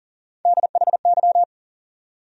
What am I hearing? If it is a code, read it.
Morse code, heard as a keyed tone, about 37 wpm, 700 Hz. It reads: DHY
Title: USA Call Sign Postfixes - 3 Letters 40wpm